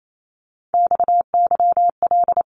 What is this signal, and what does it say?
Morse code, heard as a keyed tone, about 28 wpm, 700 Hz. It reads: XYL
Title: Abbreviations 28wpm